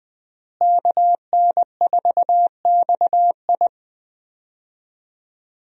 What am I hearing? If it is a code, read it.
Morse code, heard as a keyed tone, about 20 wpm, 700 Hz. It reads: KN4XI
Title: USA Call Signs - 5 Characters - Repeated 3x - 20wpm